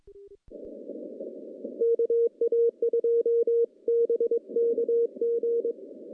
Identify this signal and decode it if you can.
Morse code, heard as a keyed tone, about 22 wpm, 470 Hz. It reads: KA2BKG
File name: contests / CW-SS-k5tr- -0100